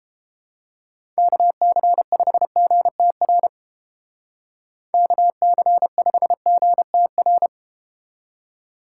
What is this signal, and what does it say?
Morse code, heard as a keyed tone, about 32 wpm, 700 Hz. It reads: KC5GTR KC5GTR